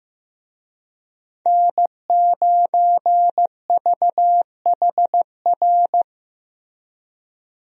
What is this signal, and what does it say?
Morse code, heard as a keyed tone, about 15 wpm, 700 Hz. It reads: N9VHR
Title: USA Call Signs - 5 Characters - Repeated 3x - 15wpm